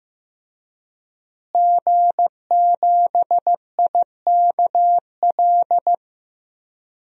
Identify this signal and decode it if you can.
Morse code, heard as a keyed tone, about 15 wpm, 700 Hz. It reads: G7IKL
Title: DXCC-100 Call Signs - 5 Characters - Repeated 2x - 15wpm